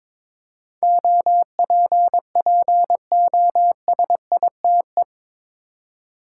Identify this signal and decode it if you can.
Morse code, heard as a keyed tone, about 22 wpm, 700 Hz. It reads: OPPOSITE
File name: practice / 22wpm / Eight Letter Words - 22wpm